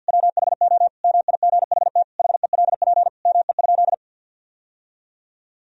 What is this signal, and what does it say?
Morse code, heard as a keyed tone, about 50 wpm, 700 Hz. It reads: WHO MIGHT HELP ME?
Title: Sentences from Top 1000 Words - Review Part 3 - 50wpm